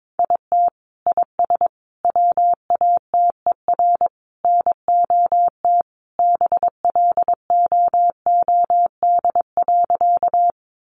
Morse code, heard as a keyed tone, about 22 wpm, 700 Hz. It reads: IT IS WATER NOT BLOOD.
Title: Sentences from Top 1100 Words 22wpm